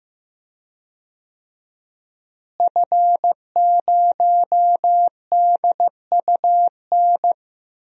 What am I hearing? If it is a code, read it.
Morse code, heard as a keyed tone, about 15 wpm, 700 Hz. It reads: F0DUN
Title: DXCC-100 Call Signs - 5 Characters - Repeated 2x - 15wpm